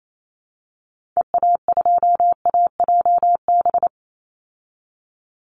Morse code, heard as a keyed tone, about 28 wpm, 700 Hz. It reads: EA2AJB
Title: Call Signs Repeated 2x - 28wpm